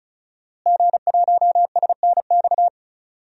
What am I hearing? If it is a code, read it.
Morse code, heard as a keyed tone, about 34 wpm, 700 Hz. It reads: G1SNX
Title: DXCC-100 Call Signs - 5 Characters - Repeated 3x - 35wpm